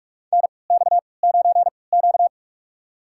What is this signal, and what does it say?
Morse code, heard as a keyed tone, about 44 wpm, spaced as for 25 wpm, 700 Hz. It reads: NX9Q
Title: USA Call Signs - 3x Character Spacing - 45wpm